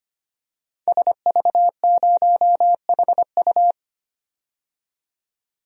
Morse code, heard as a keyed tone, about 25 wpm, 700 Hz. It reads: SV0HU
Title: DXCC-100 Call Signs - 5 Characters - 25wpm